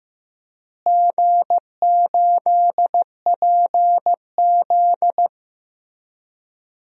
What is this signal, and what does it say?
Morse code, heard as a keyed tone, about 15 wpm, 700 Hz. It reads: G8PZ
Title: DXCC-100 Call Signs - 4 Characters - Repeated 3x - 15wpm